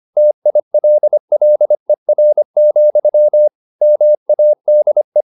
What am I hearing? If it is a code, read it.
Morse code, heard as a keyed tone, about 25 wpm, 600 Hz. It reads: TILLER, MADE